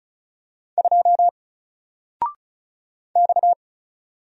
Morse code, heard as a keyed tone, about 36 wpm, 700 Hz. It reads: 2 X
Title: Single Character (Letter:Number:Punctuation:Prosign) Alt 35wpm